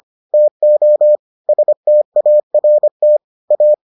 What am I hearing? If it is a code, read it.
Morse code, heard as a keyed tone, about 25 wpm, 600 Hz. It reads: TO START A